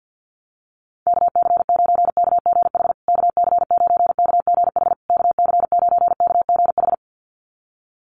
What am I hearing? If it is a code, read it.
Morse code, heard as a keyed tone, about 49 wpm, 700 Hz. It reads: KC9KGH KC9KGH KC9KGH
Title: USA Call Signs Repeated 3x 50wpm